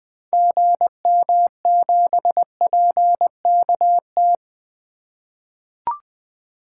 Morse code, heard as a keyed tone, about 20 wpm, 700 Hz. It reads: GM7PKT
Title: 2023 SOTA Activator Call Signs - 20wpm